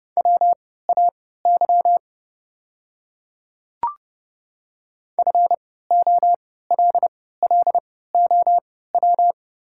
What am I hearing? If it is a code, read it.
Morse code, heard as a keyed tone, about 30 wpm, spaced as for 17 wpm, 700 Hz. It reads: WAY FOLLOW
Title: 3x Character Spacing - Top 200 Words - 30wpm